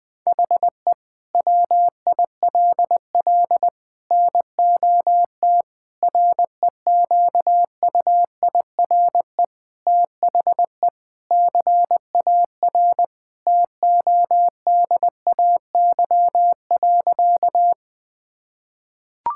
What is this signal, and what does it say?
Morse code, heard as a keyed tone, about 20 wpm, 700 Hz. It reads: HE WILL NOT REQUIRE THE CAR TODAY.
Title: Sentences from Top 1000 Words - Review Part 6 - 20wpm